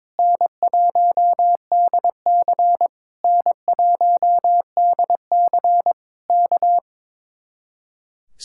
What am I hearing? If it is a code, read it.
Morse code, heard as a keyed tone, about 22 wpm, 700 Hz. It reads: N1DC N1DC K
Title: Speed-Racing Common QSO Phrases Encore - 22wpm